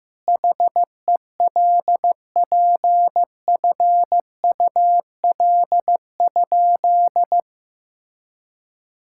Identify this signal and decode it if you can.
Morse code, heard as a keyed tone, about 15 wpm, 700 Hz. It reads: HELPFUL?
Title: Sentences from Top 1000 Words - Review Part 6 - 15wpm